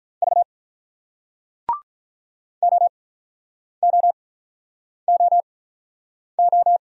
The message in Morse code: V OOOO

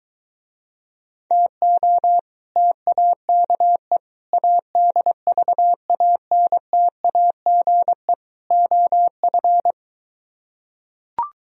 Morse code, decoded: TO TAKE ADVANTAGE OF

